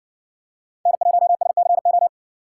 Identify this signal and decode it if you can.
Morse code, heard as a keyed tone, about 57 wpm, 700 Hz. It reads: N1SCO